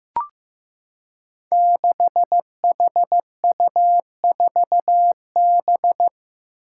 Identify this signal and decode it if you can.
Morse code, heard as a keyed tone, about 15 wpm, 700 Hz. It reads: 6HU4B
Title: Five Challenging Letter-Number Combinations 15wpm